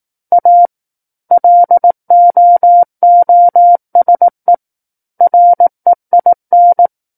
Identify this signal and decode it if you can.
Morse code, heard as a keyed tone, about 18 wpm, 700 Hz. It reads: A LOOSE REIN
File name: Book One / 09-Chapter 9 - 18 WPM